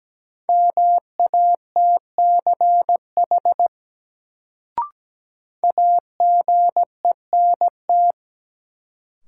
Morse code, heard as a keyed tone, about 17 wpm, 700 Hz. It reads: MATCH AGENT
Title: Sentences from Top 1600 Words - New Word Drill Alt - 17wpm